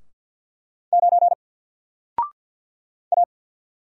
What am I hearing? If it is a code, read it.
Morse code, heard as a keyed tone, about 50 wpm, 700 Hz. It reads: 9 A
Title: Single Letter-Number Alt 50wpm